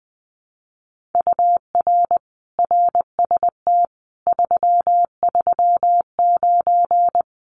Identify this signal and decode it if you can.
Morse code, heard as a keyed tone, about 20 wpm, 700 Hz. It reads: UR RST 339
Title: Common QSO Phrases Encore 20wpm